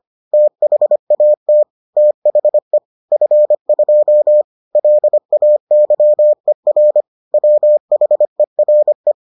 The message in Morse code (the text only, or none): THAT THE F2 LAYER WHERE